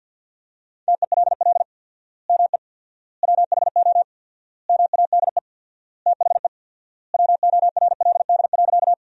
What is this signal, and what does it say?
Morse code, heard as a keyed tone, about 50 wpm, 700 Hz. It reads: TELL ME WHO MADE THE WORLD.
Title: 4x Word-Spacing - Sentences from Top 200 Words - 50wpm